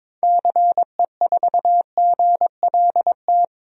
Morse code, heard as a keyed tone, about 21 wpm, 700 Hz. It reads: CE4GLT
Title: DXCC-100 Call Signs - 6 Characters - 22wpm